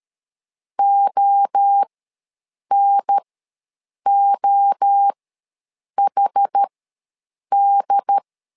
Morse code, heard as a keyed tone, about 13 wpm, spaced as for 7 wpm, 800 Hz. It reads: ONOHD
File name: LEZIONI / 3 MWRO / 3 MWRO ws 12wpm 1914